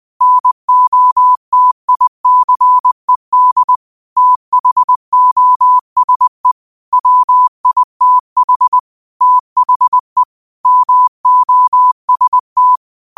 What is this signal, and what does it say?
Morse code, heard as a keyed tone, about 20 wpm, 1000 Hz. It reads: NOTICED THOSE WITH THE MOST
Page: New quotes every day in morse code at 20 Words per minute.